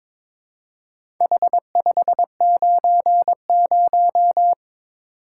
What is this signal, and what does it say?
Morse code, heard as a keyed tone, about 22 wpm, 700 Hz. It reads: H590
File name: Four Letter-Number Combinations - 22wpm